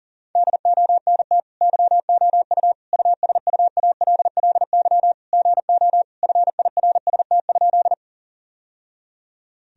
Morse code, heard as a keyed tone, about 40 wpm, 700 Hz. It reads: DONT YOU USUALLY GO FIRST?
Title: Sentences from Top 1000 Words - Review Part 3 - 40wpm